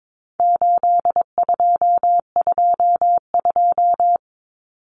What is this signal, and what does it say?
Morse code, heard as a keyed tone, about 22 wpm, 700 Hz. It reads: 8222
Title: Common Cut Numbers - 4-Digit 22wpm